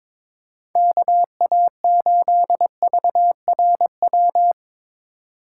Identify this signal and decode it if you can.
Morse code, heard as a keyed tone, about 22 wpm, 700 Hz. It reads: KA8VRW
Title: USA Call Signs - 6 Characters - 4x Character Spacing - 22wpm